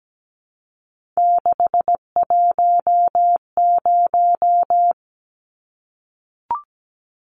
Morse code, heard as a keyed tone, about 17 wpm, 700 Hz. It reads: 610
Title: Numbers - 3-Digit 17wpm